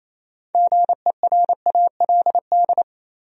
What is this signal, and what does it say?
Morse code, heard as a keyed tone, about 28 wpm, 700 Hz. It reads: GERALD
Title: USA Common Names - 28wpm